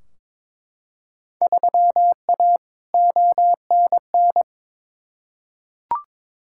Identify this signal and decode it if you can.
Morse code, heard as a keyed tone, about 22 wpm, 700 Hz. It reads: 3A ONN
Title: Field Day Exchanges Encore - 22wpm